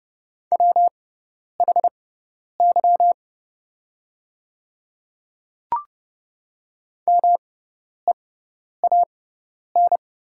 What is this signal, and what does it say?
Morse code, heard as a keyed tone, about 30 wpm, spaced as for 10 wpm, 690 Hz. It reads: WHY MEAN
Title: 6x Character Spacing - Top 500 Words - 30wpm